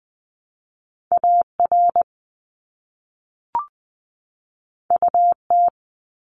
Morse code, heard as a keyed tone, about 20 wpm, 700 Hz. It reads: AR UT
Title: ARRL Sections 20wpm